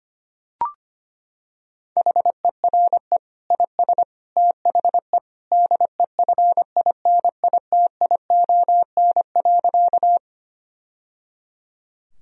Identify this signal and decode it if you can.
Morse code, heard as a keyed tone, about 25 wpm, 700 Hz. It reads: HERE IS THE DEFINITION.
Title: Sentences from Top 1000 Words - Review Part 4 - 25wpm